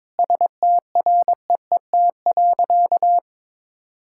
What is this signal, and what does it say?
Morse code, heard as a keyed tone, about 21 wpm, 700 Hz. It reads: STREET.
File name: practice / 22wpm / Sentences with 4-Word Phrases - 22wpm